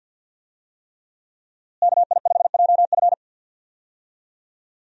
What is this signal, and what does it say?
Morse code, heard as a keyed tone, about 46 wpm, 700 Hz. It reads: KI5JF
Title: USA Call Signs - 5 Characters - Repeated 3x - 50wpm